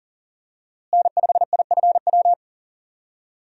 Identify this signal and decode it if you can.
Morse code, heard as a keyed tone, about 40 wpm, 700 Hz. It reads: N5IFW
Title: USA Call Signs - 5 Characters - Repeated 2x - 40wpm